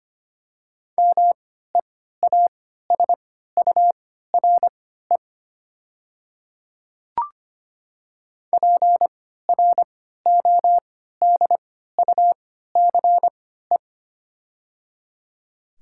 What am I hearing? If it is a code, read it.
Morse code, heard as a keyed tone, about 25 wpm, spaced as for 14 wpm, 700 Hz. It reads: MEASURE PRODUCE
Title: 3x Character Spacing - Top 500 Words - 25wpm